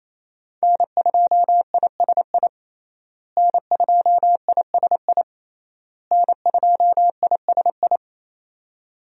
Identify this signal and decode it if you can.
Morse code, heard as a keyed tone, about 28 wpm, 700 Hz. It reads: N2ISI N2ISI N2ISI
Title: USA Call Signs - 5 Characters - Repeated 3x - 28wpm